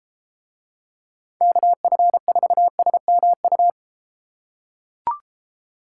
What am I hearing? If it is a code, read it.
Morse code, heard as a keyed tone, about 32 wpm, 700 Hz. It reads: KF4SMU